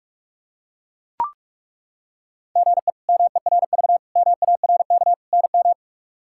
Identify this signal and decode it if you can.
Morse code, heard as a keyed tone, about 42 wpm, 700 Hz. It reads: GE MERV MARK NM
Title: K1USN SST QSO Practice - 45wpm